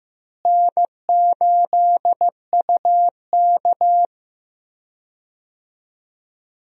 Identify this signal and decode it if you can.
Morse code, heard as a keyed tone, about 15 wpm, 700 Hz. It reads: N8UK
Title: USA Call Signs - 4 Characters - Repeated 3x - 15wpm